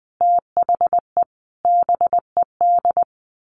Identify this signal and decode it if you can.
Morse code, heard as a keyed tone, about 20 wpm, 700 Hz. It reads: THE BED
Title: Sets of 5 Words Encore2 20wpm